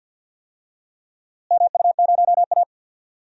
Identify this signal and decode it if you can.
Morse code, heard as a keyed tone, about 47 wpm, 700 Hz. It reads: MU0A
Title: DXCC-100 Call Signs - 4 Characters - Repeated 3x - 50wpm